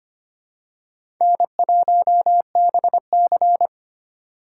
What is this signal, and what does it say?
Morse code, heard as a keyed tone, about 25 wpm, 700 Hz. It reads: N1BC
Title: USA Call Signs - 4 Characters - 4x Character Spacing - 25wpm